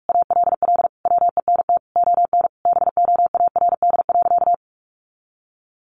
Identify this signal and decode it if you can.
Morse code, heard as a keyed tone, about 44 wpm, 700 Hz. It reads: ALL WENT ON BOARD.
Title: Sentences from Top 1400 Words 45wpm